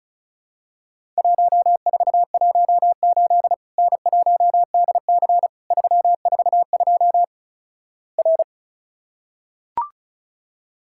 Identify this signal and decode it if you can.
Morse code, heard as a keyed tone, about 36 wpm, 700 Hz. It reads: 1418 N1DC 342 R
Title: Worked All Europe DX Contest - QTC Exchange - 35wpm